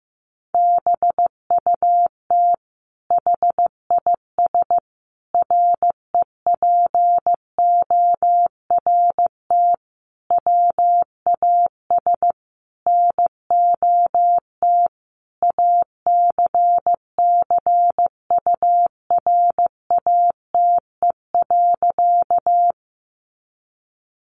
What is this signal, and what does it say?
Morse code, heard as a keyed tone, about 15 wpm, 700 Hz. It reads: BUT HIS REPORT WAS NOT ACCURATE.
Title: Sentences from Top 1200 Words 15wpm